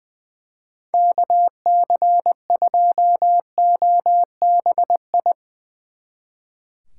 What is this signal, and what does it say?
Morse code, heard as a keyed tone, about 20 wpm, 700 Hz. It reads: KC2OBI
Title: Speed Racing - USA 2x3 Call Signs 20wpm